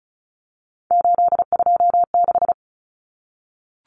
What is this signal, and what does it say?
Morse code, heard as a keyed tone, about 34 wpm, 700 Hz. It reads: 826
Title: Common Cut Numbers - 3-Digit 35wpm